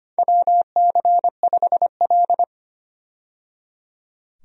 Morse code, heard as a keyed tone, about 25 wpm, 700 Hz. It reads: WC5L
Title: USA Call Signs - 4 Characters - 25wpm